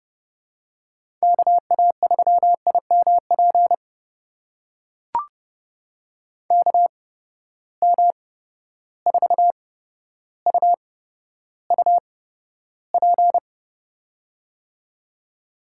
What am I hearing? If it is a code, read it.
Morse code, heard as a keyed tone, about 30 wpm, spaced as for 13 wpm, 700 Hz. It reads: KA3IMP KM4UUP
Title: USA Call Signs - 6 Characters - 8x Character Spacing - 30wpm